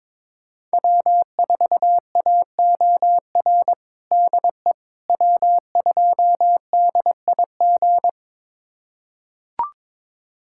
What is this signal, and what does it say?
Morse code, heard as a keyed tone, about 22 wpm, 700 Hz. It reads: W4AOR DE W2DIG
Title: USA Callsign DE Callsign - 5 Characters - 22wpm